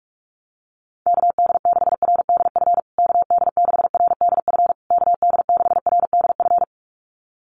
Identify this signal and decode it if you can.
Morse code, heard as a keyed tone, about 44 wpm, 700 Hz. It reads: KD6RDF KD6RDF KD6RDF
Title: USA Call Signs Repeated 3x 45wpm